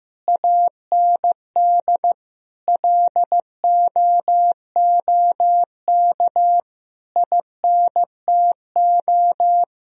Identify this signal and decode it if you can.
Morse code, heard as a keyed tone, about 15 wpm, 700 Hz. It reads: AND LOOK INTO
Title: Sentences from Top 1000 Words - Review Part 1 - 15wpm